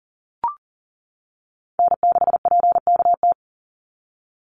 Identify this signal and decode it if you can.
Morse code, heard as a keyed tone, about 40 wpm, 700 Hz. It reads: N6PKT
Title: NA SOTA Chaser Callsigns 40wpm